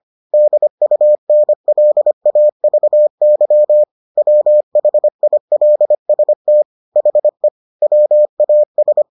Morse code, heard as a keyed tone, about 25 wpm, 600 Hz. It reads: DUNLAVY WHILST HE WAS